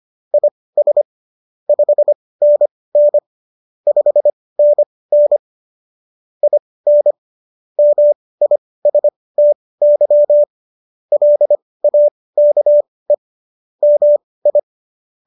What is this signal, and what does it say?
Morse code, heard as a keyed tone, about 25 wpm, spaced as for 18 wpm, 590 Hz. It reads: IS 5NN 5NN IN MISTY LAKE MI
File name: qso223 18